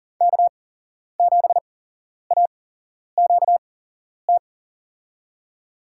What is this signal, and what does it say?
Morse code, heard as a keyed tone, about 37 wpm, spaced as for 11 wpm, 700 Hz. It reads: K7AQT
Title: USA Call Signs - 8x Character Spacing - 40wpm